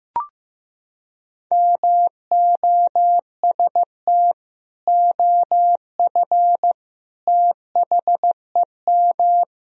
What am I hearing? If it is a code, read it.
Morse code, heard as a keyed tone, about 15 wpm, 700 Hz. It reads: MOST OF THEM
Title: Sentences from Top 1000 Words - Review Part 4 - 15wpm